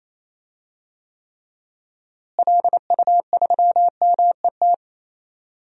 Morse code, heard as a keyed tone, about 28 wpm, 700 Hz. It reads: LU3MET
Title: DXCC-100 Call Signs - 6 Characters - Repeated 3x - 28wpm